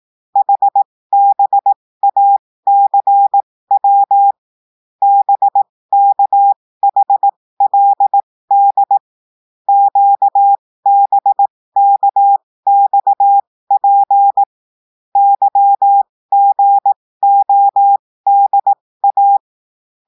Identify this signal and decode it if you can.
Morse code, heard as a keyed tone, about 18 wpm, spaced as for 15 wpm, 800 Hz. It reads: HBACW BKHLD QBKXP YGODA
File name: cours-du-lundi files / cours-1580-texte-chiffre-15-mpm